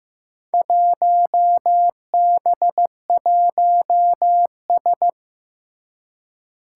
Morse code, heard as a keyed tone, about 15 wpm, 700 Hz. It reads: 1B1S
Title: Four Challenging Letter-Number Combinations 15wpm